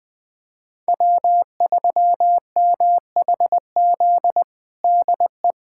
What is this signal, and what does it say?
Morse code, heard as a keyed tone, about 20 wpm, 700 Hz. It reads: W3MHZ DE
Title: USA Callsign DE Callsign - 5 Characters - 20wpm